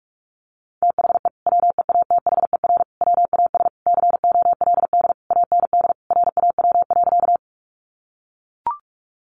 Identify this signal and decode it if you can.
Morse code, heard as a keyed tone, about 44 wpm, 700 Hz. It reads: THE WEATHER WAS COLD AND RAW.
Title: Sentences from Top 1500 Words 45wpm